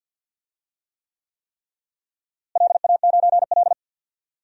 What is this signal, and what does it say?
Morse code, heard as a keyed tone, about 47 wpm, 700 Hz. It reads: LA9L
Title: DXCC-100 Call Signs - 4 Characters - Repeated 3x - 50wpm